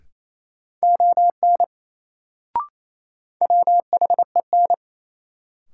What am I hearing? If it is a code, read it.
Morse code, heard as a keyed tone, about 28 wpm, 700 Hz. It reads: ON WHEN
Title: Sentences from Top 100 Words - New Word Drill - 28wpm